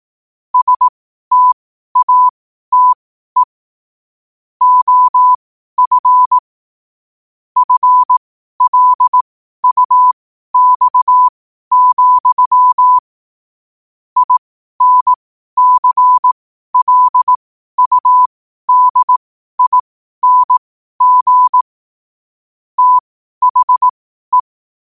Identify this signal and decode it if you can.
Morse code, heard as a keyed tone, about 18 wpm, spaced as for 12 wpm, 1000 Hz. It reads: STATE OF FLUX, INCLUDING THE
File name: qotdfiles / 12wpm